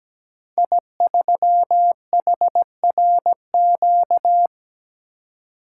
Morse code, heard as a keyed tone, about 17 wpm, 700 Hz. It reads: I3HRQ
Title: Code Groups - 5 Letter:Numbers - 17wpm